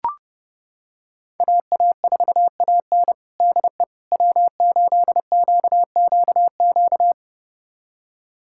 Morse code, heard as a keyed tone, about 32 wpm, 700 Hz. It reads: AA4AN DE W8QQQ
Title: USA Callsign DE Callsign 30wpm